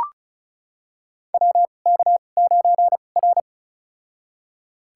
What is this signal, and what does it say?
Morse code, heard as a keyed tone, about 36 wpm, spaced as for 25 wpm, 700 Hz. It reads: WK9R
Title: USA Call Signs - 2x Character Spacing - 35wpm